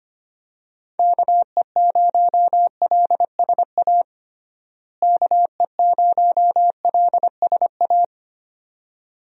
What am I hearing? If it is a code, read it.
Morse code, heard as a keyed tone, about 25 wpm, 700 Hz. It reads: KE0LSA KE0LSA
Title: USA Call Signs - 6 Characters - Repeated 2x - 25wpm